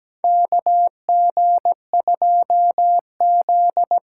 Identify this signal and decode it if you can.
Morse code, heard as a keyed tone, about 17 wpm, 700 Hz. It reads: KG2Z
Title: USA Call Signs - 8x Character Spacing - 17wpm